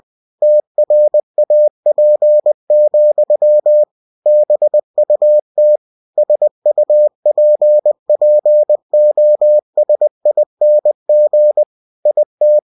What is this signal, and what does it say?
Morse code, heard as a keyed tone, about 20 wpm, 600 Hz. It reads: TRAP, BUT SUPPOSING IT
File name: cw-books / winnie-the-pooh / winnie-20wpm-page-0035